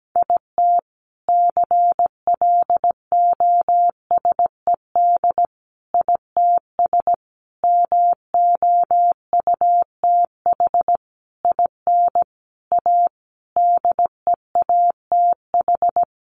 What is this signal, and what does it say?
Morse code, heard as a keyed tone, about 17 wpm, 700 Hz. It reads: IT CLOSED ITS MOUTH IN A DEATH
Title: A Sound Of Thunder 17wpm